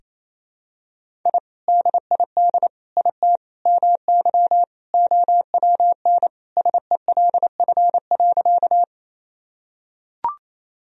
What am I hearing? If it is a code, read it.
Morse code, heard as a keyed tone, about 28 wpm, 700 Hz. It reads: I DID IT MY OWN SELF.
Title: Sentences from Top 1000 Words - Review Part 1 - 28wpm